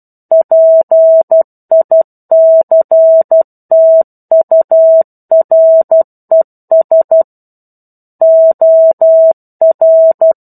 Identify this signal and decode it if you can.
Morse code, heard as a keyed tone, about 12 wpm, 650 Hz. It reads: PICTURES OR
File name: 01-Chapter-01 12 WPM